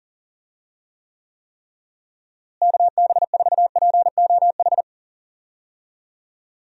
Morse code, heard as a keyed tone, about 40 wpm, 700 Hz. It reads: KB4POH
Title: USA Call Signs - 6 Characters - Repeated 3x - 40wpm